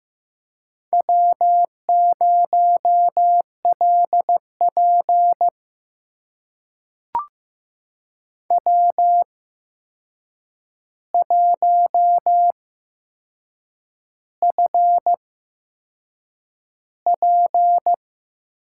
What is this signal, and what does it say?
Morse code, heard as a keyed tone, about 15 wpm, spaced as for 6 wpm, 700 Hz. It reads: W0LP W1FP
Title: USA Call Signs - 4 Characters - 8x Character Spacing - 15wpm